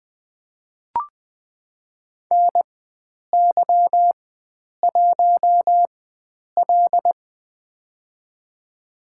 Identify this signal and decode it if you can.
Morse code, heard as a keyed tone, about 21 wpm, spaced as for 9 wpm, 700 Hz. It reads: NY1L
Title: USA Call Signs - 4 Characters - 4x Character Spacing - 20wpm